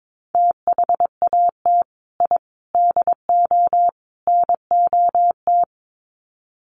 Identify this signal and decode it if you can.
Morse code, heard as a keyed tone, about 22 wpm, 700 Hz. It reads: THAT I DO NOT
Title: Sets of 4 Words 22wpm